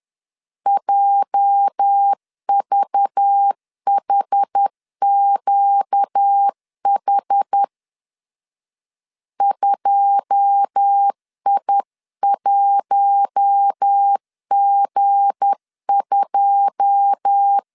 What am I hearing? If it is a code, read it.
Morse code, heard as a keyed tone, about 10 wpm, 800 Hz. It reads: JVHQH 2I1G2